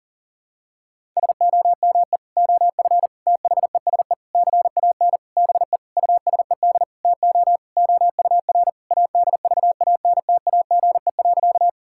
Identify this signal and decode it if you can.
Morse code, heard as a keyed tone, about 40 wpm, 700 Hz. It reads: SOME OF THESE CAN BE USED TO OUR ADVANTAGE.
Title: Sentences from Top 1000 Words - Review Part 6 - 40wpm